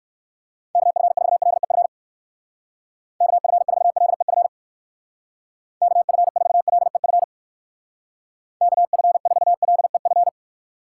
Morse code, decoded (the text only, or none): KF4LEF KF4LEF KF4LEF KF4LEF